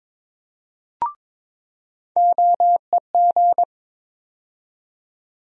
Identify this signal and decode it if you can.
Morse code, heard as a keyed tone, about 22 wpm, 700 Hz. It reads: OEG
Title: USA Call Sign Postfixes - 3 Letters 22wpm